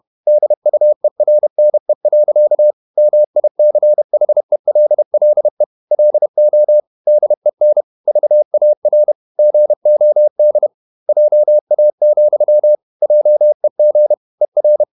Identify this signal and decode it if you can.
Morse code, heard as a keyed tone, about 31 wpm, 600 Hz. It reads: DUERNE. MICHELLE LO DEN VAR GOD JA, JEG ER
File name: Ferie QRQ 31wpm